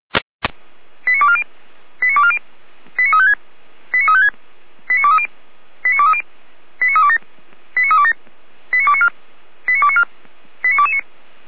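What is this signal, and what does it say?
Five-tone selective call (ZVEI), heard as quick runs of five tones, tones of about 70 ms. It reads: 89240 89240 89356 89356 89230 89230 89327 89327 89294 89294 89229
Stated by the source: Five-tone ZVEI selective call 89223 received by SDR, on the feuerwehr-orlamuende.org server.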